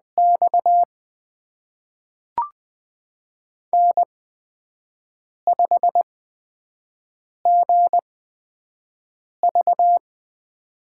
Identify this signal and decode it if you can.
Morse code, heard as a keyed tone, about 20 wpm, spaced as for 5 wpm, 700 Hz. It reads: X N5GV
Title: USA Call Signs - 4 Characters - 8x Character Spacing - 20wpm